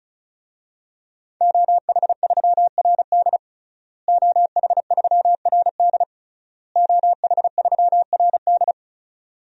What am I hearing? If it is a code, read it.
Morse code, heard as a keyed tone, about 36 wpm, 700 Hz. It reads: OH3RD OH3RD OH3RD
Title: DXCC-100 Call Signs - 5 Characters - Repeated 3x - 35wpm